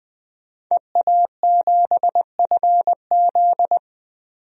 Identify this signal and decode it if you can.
Morse code, heard as a keyed tone, about 21 wpm, 700 Hz. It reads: EA7FZ
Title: DXCC-100 Call Signs - 5 Characters - 20wpm